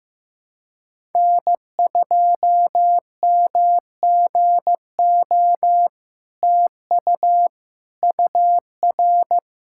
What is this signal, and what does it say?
Morse code, heard as a keyed tone, about 15 wpm, 700 Hz. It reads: N2MGO TU UR
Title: SOTA QSO Practice 15wpm